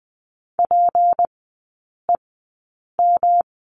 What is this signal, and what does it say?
Morse code, heard as a keyed tone, about 20 wpm, spaced as for 8 wpm, 700 Hz. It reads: PEM
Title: 152 Lesson 25e - Letter p - Three Characters 20wpm